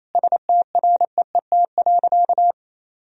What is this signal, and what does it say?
Morse code, heard as a keyed tone, about 28 wpm, 700 Hz. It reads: STREET.
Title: Sentences from Top 1000 Words - Review Part 3 - 28wpm